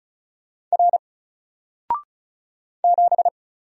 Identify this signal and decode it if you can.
Morse code, heard as a keyed tone, about 36 wpm, 700 Hz. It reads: R 7
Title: Single Letter-Number Alt 35wpm